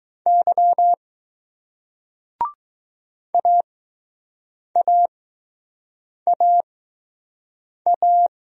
Morse code, heard as a keyed tone, about 20 wpm, spaced as for 6 wpm, 700 Hz. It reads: Y AAAA